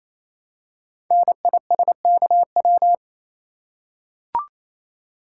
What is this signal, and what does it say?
Morse code, heard as a keyed tone, about 29 wpm, 700 Hz. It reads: NISKW
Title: Code Groups - 5 Letters - 28wpm